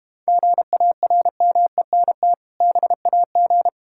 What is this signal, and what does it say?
Morse code, heard as a keyed tone, about 32 wpm, 690 Hz. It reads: GARMENT BAG